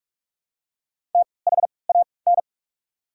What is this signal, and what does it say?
Morse code, heard as a keyed tone, about 42 wpm, 690 Hz. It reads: T H A N
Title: 3x Character Spacing - Top 100 Words - 45wpm